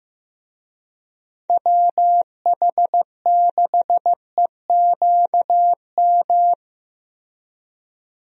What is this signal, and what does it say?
Morse code, heard as a keyed tone, about 15 wpm, 700 Hz. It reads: WH6EQM
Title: USA Call Signs - 6 Characters - Repeated 2x - 15wpm